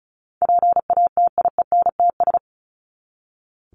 Morse code, heard as a keyed tone, about 34 wpm, 700 Hz. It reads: PATIENTS
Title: ES:S Suffix - Top 100 Words with Base Words 35wpm